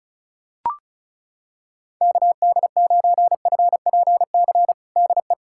Morse code, heard as a keyed tone, about 35 wpm, 700 Hz. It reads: KD9FPC DE
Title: USA Callsign DE Callsign - 6 Characters - 35wpm